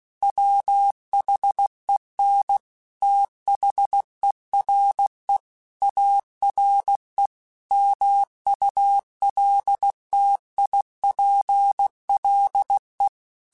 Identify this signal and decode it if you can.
Morse code, heard as a keyed tone, about 16 wpm, 800 Hz. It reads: WHEN THERE ARE MULTIPLE
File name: Technical Subjects / Technical Subjects-mp3s / AND Gate @15